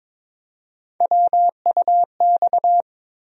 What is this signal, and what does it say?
Morse code, heard as a keyed tone, about 22 wpm, 700 Hz. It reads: WUX
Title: USA Call Sign Postfixes - 3 Letters 22wpm